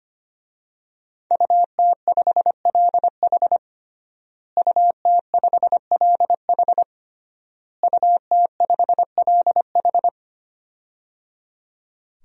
Morse code, heard as a keyed tone, about 25 wpm, 700 Hz. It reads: UT5LH UT5LH UT5LH
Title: DXCC-100 Call Signs - 5 Characters - Repeated 3x - 25wpm